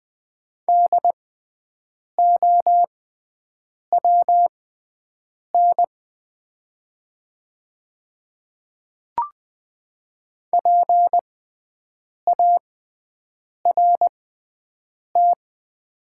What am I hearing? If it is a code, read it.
Morse code, heard as a keyed tone, about 20 wpm, spaced as for 7 wpm, 700 Hz. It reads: DOWN PART
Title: 6x Character Spacing - Top 200 Words - 20wpm